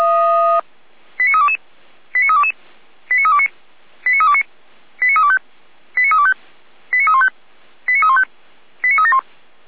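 Five-tone selective call (ZVEI), heard as quick runs of five tones, tones of about 70 ms. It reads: 89322 89322 89329 89329 89325 89325 89315 89315 89371
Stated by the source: ZVEI five-tone call 89109 received by SDR.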